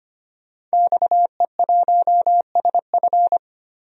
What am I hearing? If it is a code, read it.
Morse code, heard as a keyed tone, about 25 wpm, 700 Hz. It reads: XE1SF
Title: DXCC-100 Call Signs - 5 Characters - Repeated 3x - 25wpm